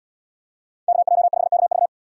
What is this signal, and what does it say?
Morse code, heard as a keyed tone, about 68 wpm, 700 Hz. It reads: KJ5KV